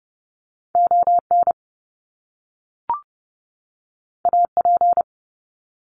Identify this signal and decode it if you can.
Morse code, heard as a keyed tone, about 30 wpm, 700 Hz. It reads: ON AP
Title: Top Two Letter Combinations in Words 30wpm